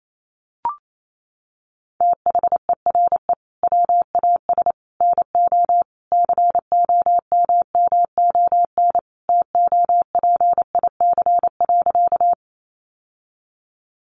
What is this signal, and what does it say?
Morse code, heard as a keyed tone, about 28 wpm, 700 Hz. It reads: THERE WAS NO COMMON TOPIC.
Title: Sentences from Top 1200 Words 28wpm